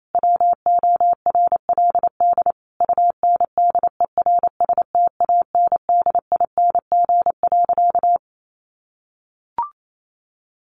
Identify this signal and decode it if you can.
Morse code, heard as a keyed tone, about 28 wpm, 700 Hz. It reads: WORLD UNDERSTANDING.
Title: Sentences from Top 1400 Words 28wpm